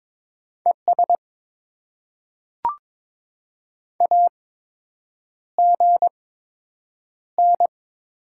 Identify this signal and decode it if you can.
Morse code, heard as a keyed tone, about 22 wpm, spaced as for 8 wpm, 700 Hz. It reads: ES AGN
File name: Top 20 QSO Elements - 8x Character Spacing - 22wpm